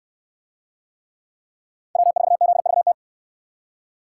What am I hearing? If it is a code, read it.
Morse code, heard as a keyed tone, about 68 wpm, 700 Hz. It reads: W4ZVT